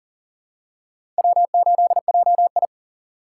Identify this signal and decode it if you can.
Morse code, heard as a keyed tone, about 38 wpm, 700 Hz. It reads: W8JI
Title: USA Call Signs - 2x Character Spacing - 40wpm